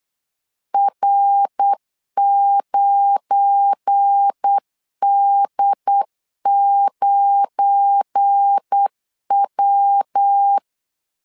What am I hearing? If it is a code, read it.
Morse code, heard as a keyed tone, about 8 wpm, 800 Hz. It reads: R9D9W
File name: LEZIONI / 7 GF9Z / 7 GF9 8wpm 3545